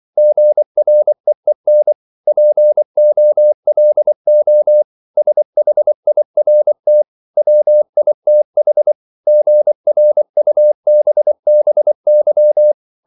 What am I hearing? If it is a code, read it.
Morse code, heard as a keyed tone, about 24 wpm, 600 Hz. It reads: GREEN POLO SHIRT WITH GRUBBY